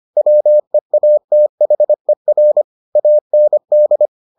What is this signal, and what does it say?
Morse code, heard as a keyed tone, about 25 wpm, 600 Hz. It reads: WEATHER AND